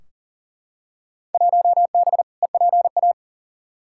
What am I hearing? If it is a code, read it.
Morse code, heard as a keyed tone, about 40 wpm, 700 Hz. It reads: 1B EPA
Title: Field Day Exchanges Encore - 40wpm